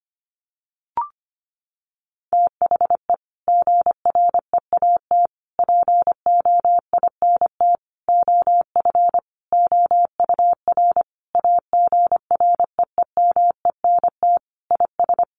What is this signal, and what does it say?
Morse code, heard as a keyed tone, about 25 wpm, 700 Hz. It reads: THE GREAT POINT OF OUR AGREEMENT IS
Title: Sentences from Top 1500 Words 25wpm